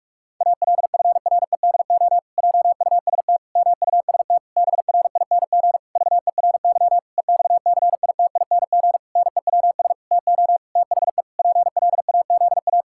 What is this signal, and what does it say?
Morse code, heard as a keyed tone, about 46 wpm, 700 Hz. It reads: ALFREDO JUST MUST BRING VERY EXCITING NEWS TO THE PLAZA
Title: Pangram Challenge v2 - 45wpm